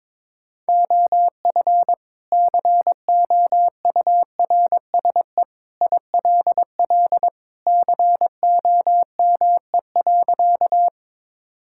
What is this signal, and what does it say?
Morse code, heard as a keyed tone, about 22 wpm, 700 Hz. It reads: OF COURSE ILL COME.
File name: practice / 22wpm / Sentences with 2-Word Phrases - 22wpm